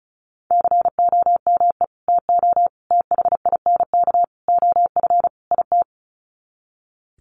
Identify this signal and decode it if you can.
Morse code, heard as a keyed tone, about 34 wpm, 700 Hz. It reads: COME TO THINK OF IT
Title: Sets of 5 Words Encore 35wpm